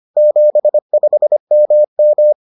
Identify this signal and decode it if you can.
Morse code, heard as a keyed tone, about 25 wpm, 600 Hz. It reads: 75MM